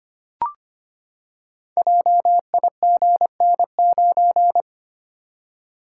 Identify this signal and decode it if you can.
Morse code, heard as a keyed tone, about 25 wpm, 700 Hz. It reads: JIGN9
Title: Code Groups - 5 Letter:Numbers - 25wpm